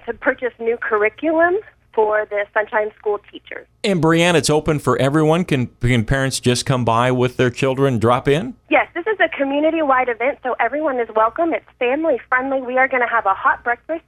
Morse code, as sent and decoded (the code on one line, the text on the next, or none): none